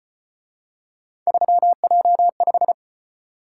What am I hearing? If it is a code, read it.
Morse code, heard as a keyed tone, about 34 wpm, 700 Hz. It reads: 3J5